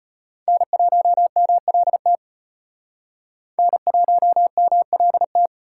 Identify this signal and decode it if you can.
Morse code, heard as a keyed tone, about 35 wpm, 700 Hz. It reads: N1MLT N1MLT